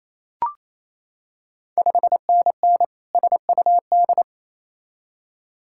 Morse code, heard as a keyed tone, about 27 wpm, 690 Hz. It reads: 5NN SUD
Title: Ontario QSO Party Exchange 28wpm